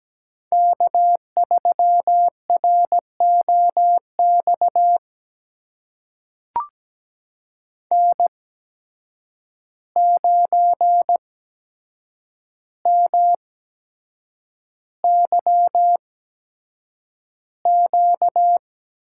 Morse code, heard as a keyed tone, about 17 wpm, spaced as for 7 wpm, 700 Hz. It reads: K3ROX N9MYQ
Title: USA Call Signs - 5 Characters - 8x Character Spacing - 17wpm